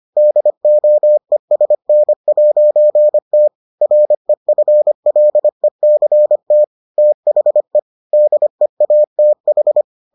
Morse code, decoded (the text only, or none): DOESN'T REFLECT THE DEATH